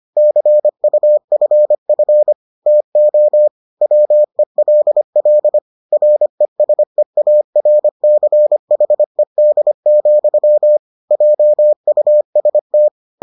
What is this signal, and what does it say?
Morse code, heard as a keyed tone, about 25 wpm, 600 Hz. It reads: CUFF TO WELL RESEARCHED, JUST